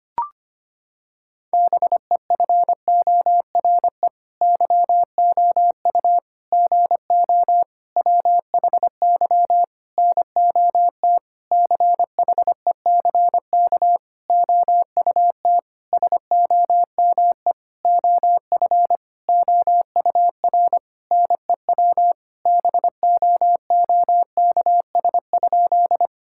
Morse code, heard as a keyed tone, about 25 wpm, 700 Hz. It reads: BEFORE YOU GO WHY NOT CHECK OUT SOME OF OUR NEW BOOKS?
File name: practice / 25wpm / Sentences with 9-Word Phrases - 25wpm